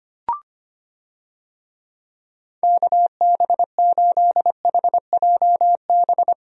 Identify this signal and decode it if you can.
Morse code, heard as a keyed tone, about 25 wpm, 700 Hz. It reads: KB8HJB
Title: USA Call Signs - 6 Characters - 25wpm